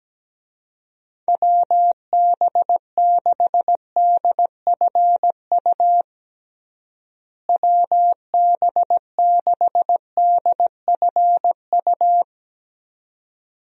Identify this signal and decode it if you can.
Morse code, heard as a keyed tone, about 17 wpm, 700 Hz. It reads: WB6DFU WB6DFU
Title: USA Call Signs - 6 Characters - Repeated 2x - 17wpm